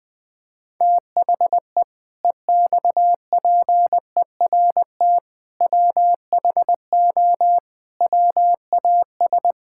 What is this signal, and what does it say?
Morse code, heard as a keyed tone, about 20 wpm, 700 Hz. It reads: THE EXPERT WHO WAS
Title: Sentences with 7-Word Phrases - Spoken Beginning - 20wpm